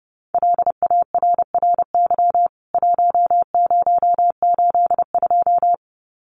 Morse code, heard as a keyed tone, about 30 wpm, 700 Hz. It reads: LARRY 1082
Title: CWT Exchange 30wpm